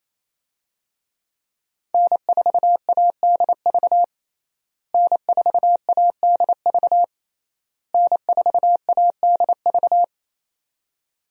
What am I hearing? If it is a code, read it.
Morse code, heard as a keyed tone, about 28 wpm, 700 Hz. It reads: N4ADV N4ADV N4ADV
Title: USA Call Signs - 5 Characters - Repeated 3x - 28wpm